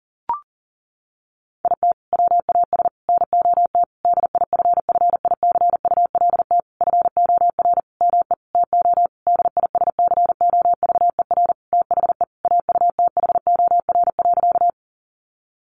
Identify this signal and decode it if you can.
Morse code, heard as a keyed tone, about 40 wpm, 700 Hz. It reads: IT WAS NOT DIFFICULT FOR ME TO DISCOVER THE AUTHOR.
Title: Sentences from Top 1200 Words 40wpm